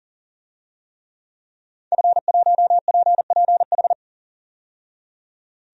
Morse code, heard as a keyed tone, about 37 wpm, 700 Hz. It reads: F1PPH
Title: DXCC-100 Call Signs - 5 Characters - 40wpm